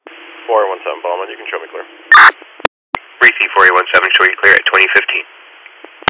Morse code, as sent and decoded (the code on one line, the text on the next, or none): none